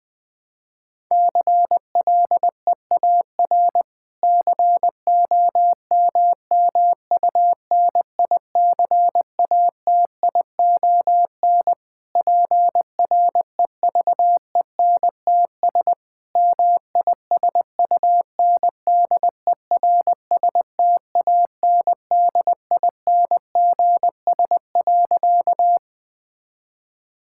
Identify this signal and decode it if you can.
Morse code, heard as a keyed tone, about 20 wpm, 700 Hz. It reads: CLEAR COMMUNICATION PREVENTS MISUNDERSTANDINGS.
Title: Difficult Sentences with Hints - 20wpm